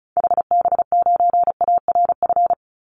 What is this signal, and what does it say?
Morse code, heard as a keyed tone, about 34 wpm, 700 Hz. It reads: HB9ARF
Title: Call Signs - 35wpm